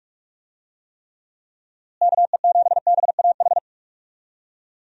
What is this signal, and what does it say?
Morse code, heard as a keyed tone, about 41 wpm, 700 Hz. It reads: KE7BAH
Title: USA Call Signs - 6 Characters - Repeated 2x - 45wpm